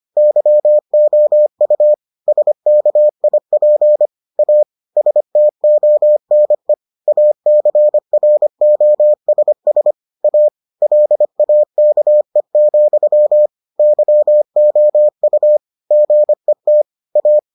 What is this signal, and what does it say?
Morse code, heard as a keyed tone, about 25 wpm, 600 Hz. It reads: YOU SKIP A STONE ACROSS A LAKE, YOU GET A